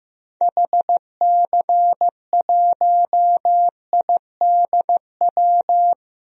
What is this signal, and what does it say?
Morse code, heard as a keyed tone, about 15 wpm, 700 Hz. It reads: HC1IDW
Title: DXCC-100 Call Signs - 6 Characters - Repeated 3x - 15wpm